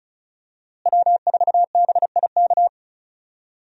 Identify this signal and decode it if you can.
Morse code, heard as a keyed tone, about 36 wpm, 700 Hz. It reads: W4BIK
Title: USA Call Signs - 5 Characters - Repeated 2x - 35wpm